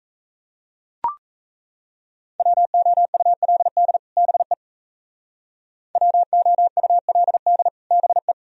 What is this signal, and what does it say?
Morse code, heard as a keyed tone, about 34 wpm, 700 Hz. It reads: WOULD BE WOULD BE
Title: Speed-Racing - Sets of 2 Words 28wpm